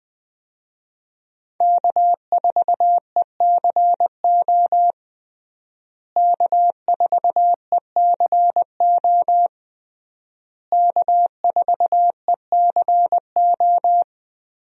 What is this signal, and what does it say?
Morse code, heard as a keyed tone, about 20 wpm, 700 Hz. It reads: K4ECO K4ECO K4ECO
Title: USA Call Signs - 5 Characters - Repeated 3x - 20wpm